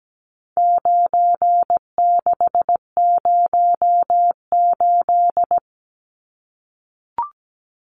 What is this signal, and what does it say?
Morse code, heard as a keyed tone, about 17 wpm, 700 Hz. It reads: 9608
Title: Numbers - 4-Digit 17wpm